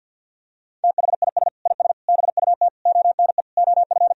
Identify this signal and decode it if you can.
Morse code, heard as a keyed tone, about 47 wpm, 700 Hz. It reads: THIS IS BUT ONE OF
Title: Sentences from Top 1000 Words - Review Part 2 - 50wpm